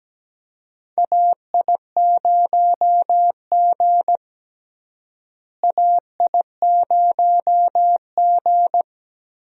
Morse code, heard as a keyed tone, about 17 wpm, 700 Hz. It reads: AI0G AI0G
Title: USA Call Signs - 4 Characters - Repeated 3x - 17wpm